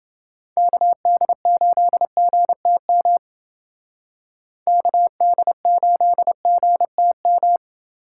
Morse code, heard as a keyed tone, about 29 wpm, 700 Hz. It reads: KD8GTM KD8GTM